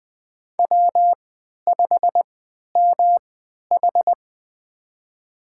Morse code, heard as a keyed tone, about 20 wpm, spaced as for 11 wpm, 700 Hz. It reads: W5MH
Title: USA Call Signs - 4 Characters - 3x Character Spacing - 20wpm